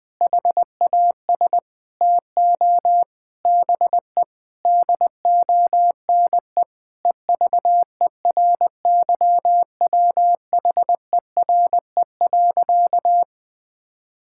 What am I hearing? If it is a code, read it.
Morse code, heard as a keyed tone, about 20 wpm, 700 Hz. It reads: HAS TO BE DONE EVERYWHERE.
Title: Sentences from Top 1600 Words 20wpm